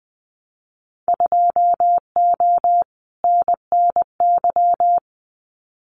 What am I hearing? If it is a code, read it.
Morse code, heard as a keyed tone, about 20 wpm, 700 Hz. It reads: 2O NNY
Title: Winter Field Day Exchanges 20wpm